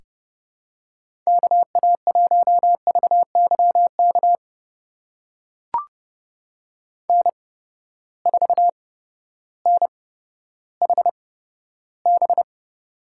Morse code, heard as a keyed tone, about 32 wpm, spaced as for 14 wpm, 700 Hz. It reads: KA1VYK N4NHB
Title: USA Call Signs - 8x Character Spacing - 30wpm